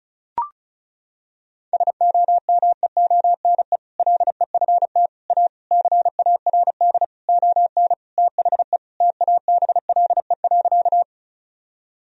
Morse code, heard as a keyed tone, about 34 wpm, 700 Hz. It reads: SOMEONE LEFT A CARD ON THE TABLE.
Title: Sentences from Top 1000 Words - Review Part 4 - 35wpm